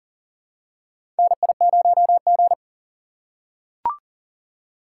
Morse code, heard as a keyed tone, about 40 wpm, 700 Hz. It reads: NI0G
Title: USA Call Signs - 4 Characters - 3x Character Spacing - 40wpm